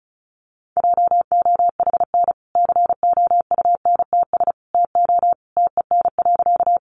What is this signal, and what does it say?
Morse code, heard as a keyed tone, about 34 wpm, 700 Hz. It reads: JOHN COUNTS TO TEN.
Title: Easy Phrases 35wpm